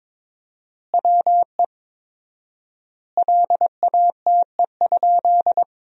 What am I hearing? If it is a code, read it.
Morse code, heard as a keyed tone, about 22 wpm, 700 Hz. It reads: WE LATE?
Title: 4x Word-Spacing - Sentences from Top 300 Words - 22wpm